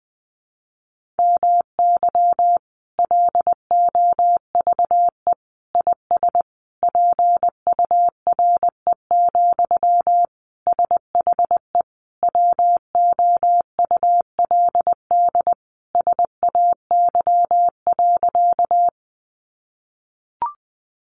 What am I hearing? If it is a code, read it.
Morse code, heard as a keyed tone, about 20 wpm, 700 Hz. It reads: MY LOVE IS PURE, SHE WOULD SAY.
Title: Sentences from Top 1500 Words 20wpm